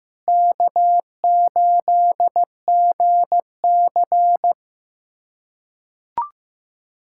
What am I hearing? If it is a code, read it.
Morse code, heard as a keyed tone, about 16 wpm, 700 Hz. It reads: K8GC
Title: USA Callsign DE Callsign - 4 Characters - 15wpm